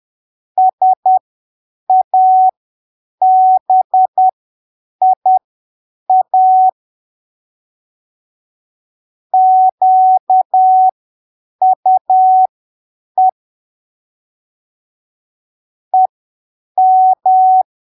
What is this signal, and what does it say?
Morse code, heard as a keyed tone, about 10 wpm, spaced as for 7 wpm, 750 Hz. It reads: SABIA QUE EM
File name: arquivos / texto / texto 16 10ppm